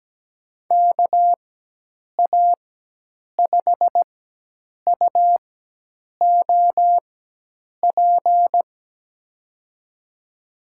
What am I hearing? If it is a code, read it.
Morse code, heard as a keyed tone, about 17 wpm, spaced as for 8 wpm, 700 Hz. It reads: KA5UOP
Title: USA Call Signs - 6 Characters - 4x Character Spacing - 17wpm